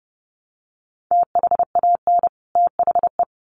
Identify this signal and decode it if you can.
Morse code, heard as a keyed tone, about 30 wpm, 700 Hz. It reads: THAN THE
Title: Sets of 2 Words 30wpm